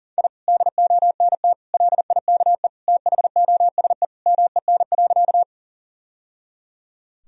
Morse code, heard as a keyed tone, about 40 wpm, 700 Hz. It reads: I DONT LIKE THOSE MEN.
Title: Sentences from Top 1000 Words - Review Part 1 - 40wpm